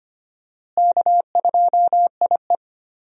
Morse code, heard as a keyed tone, about 25 wpm, 700 Hz. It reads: K2IE
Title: USA Call Signs - 4 Characters - Repeated 3x - 25wpm